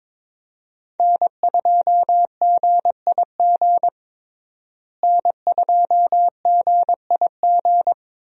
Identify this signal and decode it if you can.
Morse code, heard as a keyed tone, about 21 wpm, 700 Hz. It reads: N2GIG N2GIG
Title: USA Call Signs - 5 Characters - Repeated 3x - 22wpm